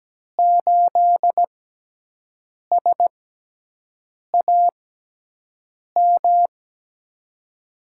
Morse code, heard as a keyed tone, about 17 wpm, spaced as for 6 wpm, 700 Hz. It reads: 8SAM
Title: USA Call Signs - 5 Characters - 6x Character Spacing - 17wpm